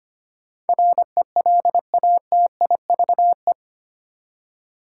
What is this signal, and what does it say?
Morse code, heard as a keyed tone, about 25 wpm, 700 Hz. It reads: RELATIVE
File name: Eight Letter Words - 25wpm